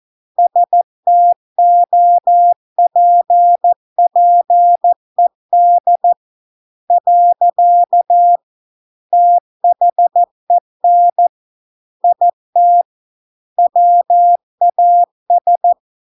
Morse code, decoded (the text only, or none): STOPPED . THEN IT WAS